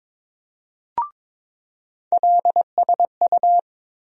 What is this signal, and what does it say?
Morse code, heard as a keyed tone, about 22 wpm, 700 Hz. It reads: LSU
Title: USA Call Sign Postfixes - 3 Letters 22wpm